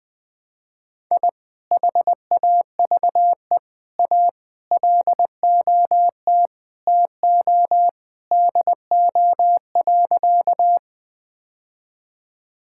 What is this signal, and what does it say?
Morse code, heard as a keyed tone, about 20 wpm, 700 Hz. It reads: I HAVE A LOT TO DO.
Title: Sentences with 2-Word Phrases - Spoken Beginning - 20wpm